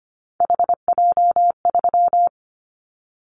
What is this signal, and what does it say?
Morse code, heard as a keyed tone, about 25 wpm, 700 Hz. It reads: HJ3
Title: Callsign Prefixes for Top 100 DXCC - 25wpm